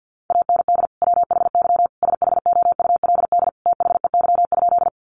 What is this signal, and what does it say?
Morse code, heard as a keyed tone, about 50 wpm, 700 Hz. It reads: AND WHY SHOULD THEY?
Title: Sentences from Top 300 Words 50wpm